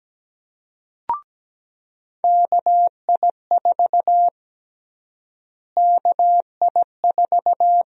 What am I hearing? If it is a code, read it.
Morse code, heard as a keyed tone, about 17 wpm, 700 Hz. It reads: KI4 KI4
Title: USA Call Sign Prefixes - Repeated 2x - 17wpm